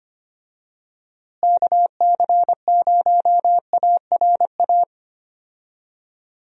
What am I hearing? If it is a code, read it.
Morse code, heard as a keyed tone, about 25 wpm, 700 Hz. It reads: KC0ARA
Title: USA Call Signs - 6 Characters - Repeated 2x - 25wpm